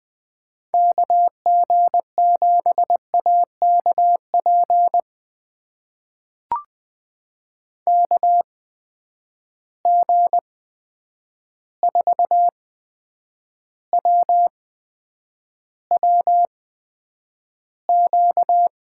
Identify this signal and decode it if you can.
Morse code, heard as a keyed tone, about 20 wpm, spaced as for 9 wpm, 700 Hz. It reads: KG7AKP KG4WWQ
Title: USA Call Signs - 6 Characters - 8x Character Spacing - 20wpm